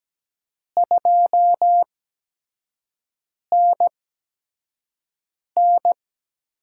Morse code, heard as a keyed tone, about 17 wpm, spaced as for 5 wpm, 700 Hz. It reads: 2NN
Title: USA Call Signs - 4 Characters - 8x Character Spacing - 17wpm